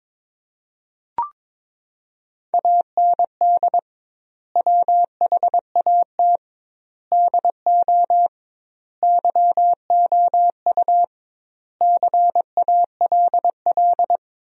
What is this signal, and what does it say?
Morse code, heard as a keyed tone, about 21 wpm, 700 Hz. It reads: AND WHAT DO YOU CALL
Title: 2x Word-Spacing - Sentences from Top 100 Words - 22wpm